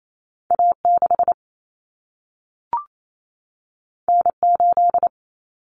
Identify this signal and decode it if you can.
Morse code, heard as a keyed tone, about 28 wpm, 700 Hz. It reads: A6 N8
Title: Common Cut Numbers - 2-Digit 28wpm